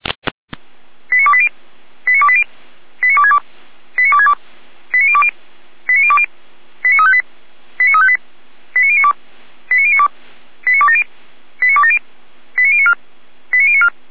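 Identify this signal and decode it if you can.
Five-tone selective call (ZVEI), heard as quick runs of five tones, tones of about 70 ms. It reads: 89280 89280 89262 89262 89020 89020 89358 89358 89092 89092 89270 89270 89094 89094
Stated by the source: Five-tone ZVEI selective call received by SDR, from the server of Feuerwehr Orlamünde.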